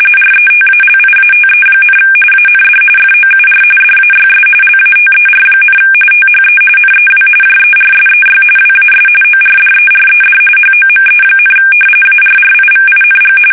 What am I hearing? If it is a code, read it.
RTTY, about 45.45 baud. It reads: RETALIATE, AND FOUR DAYS LATER, THE NEW JERSEY WAS ORDERED HOME, ARRIVING
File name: fsk850 / battleshipnj